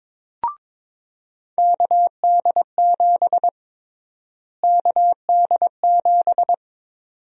Morse code, heard as a keyed tone, about 22 wpm, 700 Hz. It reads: KD7 KD7
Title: USA Call Sign Prefixes - Repeated 2x - 22wpm